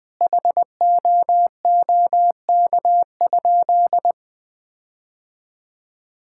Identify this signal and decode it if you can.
Morse code, heard as a keyed tone, about 20 wpm, 700 Hz. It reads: HOOK?
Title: Sentences from Top 1600 Words 20wpm